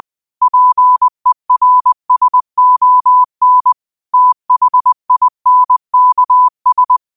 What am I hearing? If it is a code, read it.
Morse code, heard as a keyed tone, about 20 wpm, 1000 Hz. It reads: PERSON THINKS